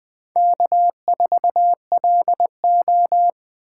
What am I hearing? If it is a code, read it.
Morse code, heard as a keyed tone, about 20 wpm, 700 Hz. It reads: K4LO
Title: USA Call Signs - 2x Character Spacing - 20wpm